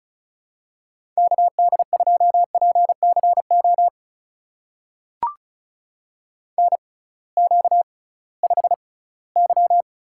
Morse code, heard as a keyed tone, about 35 wpm, spaced as for 21 wpm, 700 Hz. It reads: KD2PCO NQ5Y
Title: USA Call Signs - 6x Character Spacing - 35wpm